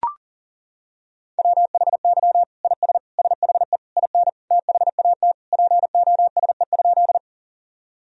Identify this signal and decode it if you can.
Morse code, heard as a keyed tone, about 40 wpm, 700 Hz. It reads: WHY IS SHE IN THAT POSE?
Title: Sentences from Top 1000 Words - Review Part 2 - 40wpm